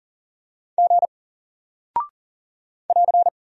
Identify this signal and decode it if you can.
Morse code, heard as a keyed tone, about 40 wpm, 700 Hz. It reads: G <AR>
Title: Single Character (Letter:Number:Punctuation:Prosign) Alt 40wpm